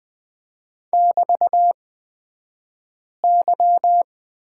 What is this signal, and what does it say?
Morse code, heard as a keyed tone, about 20 wpm, spaced as for 5 wpm, 700 Hz. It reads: <BT>Y
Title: 223 Lesson 36d - BT - Two Characters 20wpm